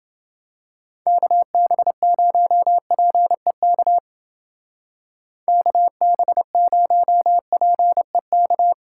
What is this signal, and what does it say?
Morse code, heard as a keyed tone, about 28 wpm, 700 Hz. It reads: KB0PEK KB0PEK